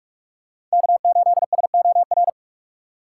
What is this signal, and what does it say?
Morse code, heard as a keyed tone, about 43 wpm, 700 Hz. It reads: K8SOR
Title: USA Call Signs - 5 Characters - 2x Character Spacing - 45wpm